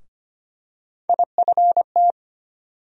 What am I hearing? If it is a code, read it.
Morse code, heard as a keyed tone, about 25 wpm, 700 Hz. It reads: IFT
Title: USA Call Sign Postfixes - 3 Letters 25wpm